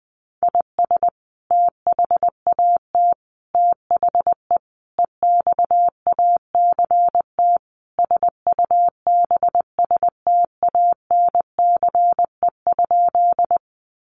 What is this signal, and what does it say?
Morse code, heard as a keyed tone, about 20 wpm, 700 Hz. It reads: IS THAT THE EXACT SUBSTANCE?
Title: Sentences from Top 1500 Words 20wpm